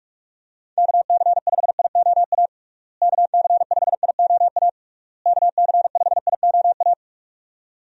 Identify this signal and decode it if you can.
Morse code, heard as a keyed tone, about 46 wpm, 700 Hz. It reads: KC5IOA KC5IOA KC5IOA
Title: USA Call Signs - 6 Characters - Repeated 3x - 45wpm